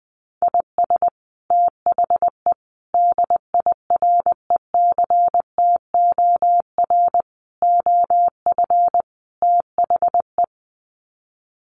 Morse code, heard as a keyed tone, about 20 wpm, 700 Hz. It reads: IS THE DIRECTOR OF THE
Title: Sets of 5 Words Encore2 20wpm